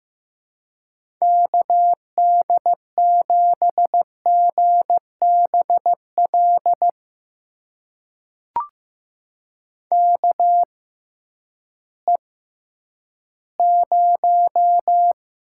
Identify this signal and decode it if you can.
Morse code, heard as a keyed tone, about 15 wpm, spaced as for 10 wpm, 700 Hz. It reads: KD7GBL KE0
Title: USA Call Signs - 6 Characters - 6x Character Spacing - 15wpm